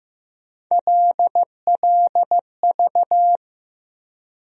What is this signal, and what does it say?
Morse code, heard as a keyed tone, about 15 wpm, 700 Hz. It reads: LLV
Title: USA Call Sign Postfixes - 3 Letters 15wpm